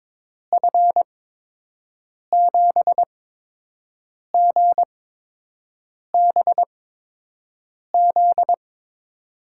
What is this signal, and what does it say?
Morse code, heard as a keyed tone, about 22 wpm, spaced as for 6 wpm, 700 Hz. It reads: F7GBZ
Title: USA Call Signs - 6 Characters - 8x Character Spacing - 22wpm